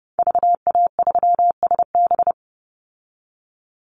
Morse code, heard as a keyed tone, about 30 wpm, 700 Hz. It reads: VA3SB
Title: Call Signs Repeated 2x - 30wpm